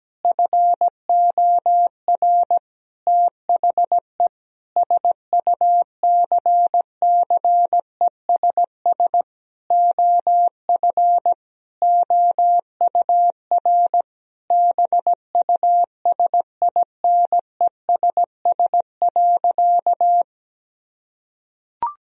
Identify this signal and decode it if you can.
Morse code, heard as a keyed tone, about 17 wpm, 700 Hz. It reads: FOR THE SUCCESS OF OUR BUSINESS.
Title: Sentences from Top 1000 Words - Review Part 4 - 17wpm